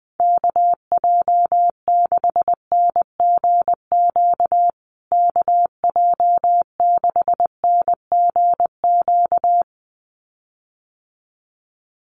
Morse code, heard as a keyed tone, about 20 wpm, 700 Hz. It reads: KJ6NGQ KJ6NGQ
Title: USA Call Signs Repeated 2x 20wpm